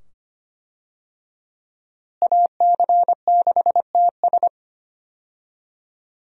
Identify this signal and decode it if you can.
Morse code, heard as a keyed tone, about 25 wpm, 700 Hz. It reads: AC6TS
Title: USA Call Signs - 5 Characters - Repeated 2x - 25wpm